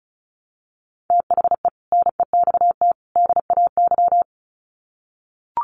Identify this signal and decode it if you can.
Morse code, heard as a keyed tone, about 34 wpm, 700 Hz. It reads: THE NEXT DAY
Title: Sets of 3 Words 35wpm